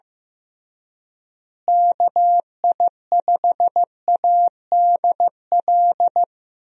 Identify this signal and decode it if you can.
Morse code, heard as a keyed tone, about 15 wpm, 700 Hz. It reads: KI5ADL
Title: USA Call Signs - 6 Characters - Repeated 3x - 15wpm